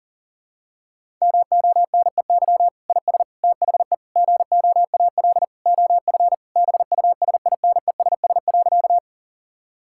Morse code, heard as a keyed tone, about 40 wpm, 700 Hz. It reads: MONEY IS THE GOAL OF BUSINESS.
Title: Sentences from Top 1000 Words - Review Part 5 - 40wpm